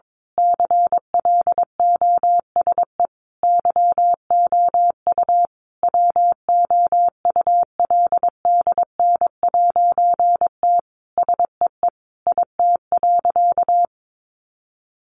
Morse code, heard as a keyed tone, about 22 wpm, 700 Hz. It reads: CLOSE YOU WOULDN'T SEE IT.
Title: A Sound Of Thunder 22wpm